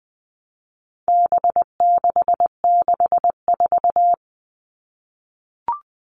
Morse code, heard as a keyed tone, about 20 wpm, 700 Hz. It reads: B664
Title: All Cut Numbers - 4-Digit 20wpm